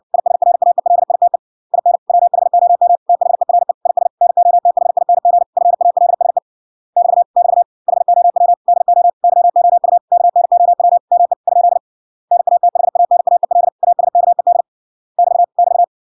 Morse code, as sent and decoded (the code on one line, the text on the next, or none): .. ... .-. .- . .-.. .. - . / .. -. / .-- .... --- -- / - .... .-. / .. ... / -. --- - .... .. -. --. / ..-. .- .-.. ... / -...- / -...- / .... --- .-- / -.. --- / -.-- --- ..- / -.- -. --- .-- / -- . / ..--.. / -. .- - .... .- -. .- . .-.. / .- ... -.- . -.. / -...- / -...-
ISRAELITE IN WHOM THERE IS NOTHING FALSE <BT> <BT> HOW DO YOU KNOW ME ? NATHANAEL ASKED <BT> <BT>